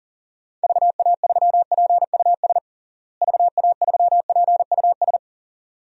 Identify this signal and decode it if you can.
Morse code, heard as a keyed tone, about 40 wpm, 700 Hz. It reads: VA3PUS VA3PUS
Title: DXCC-100 Call Signs - 6 Characters - Repeated 2x - 40wpm